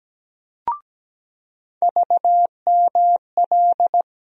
Morse code, heard as a keyed tone, about 17 wpm, 700 Hz. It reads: VML
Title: USA Call Sign Postfixes - 3 Letters 17wpm